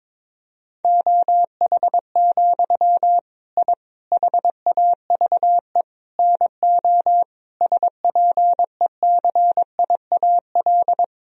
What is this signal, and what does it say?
Morse code, heard as a keyed tone, about 22 wpm, 700 Hz. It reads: OH, I HAVE NO SPECIAL
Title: Sentences from Top 1000 Words - Review Part 3 - 22wpm